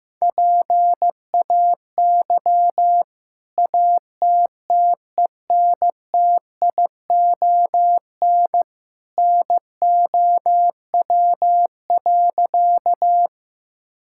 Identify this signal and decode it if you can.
Morse code, heard as a keyed tone, about 16 wpm, 700 Hz. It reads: PAY ATTENTION NOW.
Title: Sentences from Top 1000 Words - Review Part 6 - 15wpm